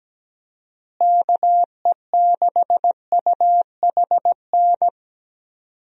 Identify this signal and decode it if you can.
Morse code, heard as a keyed tone, about 17 wpm, 700 Hz. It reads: KE6UHN
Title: USA Call Signs - 6 Characters - Repeated 3x - 17wpm